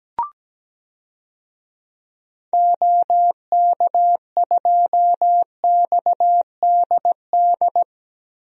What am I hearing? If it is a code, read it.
Morse code, heard as a keyed tone, about 17 wpm, 700 Hz. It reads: OK2XDD
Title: DXCC-100 Call Signs - 6 Characters - Repeated 2x - 17wpm